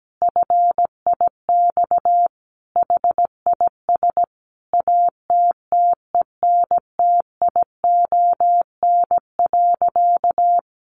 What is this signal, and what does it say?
Morse code, heard as a keyed tone, about 17 wpm, 700 Hz. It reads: FIX HIS ATTENTION.
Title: Sentences from Top 1400 Words 17wpm